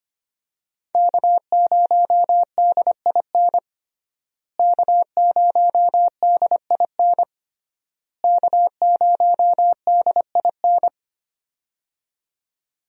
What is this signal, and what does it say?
Morse code, heard as a keyed tone, about 25 wpm, 700 Hz. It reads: K0DIN K0DIN K0DIN
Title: USA Call Signs - 5 Characters - Repeated 3x - 25wpm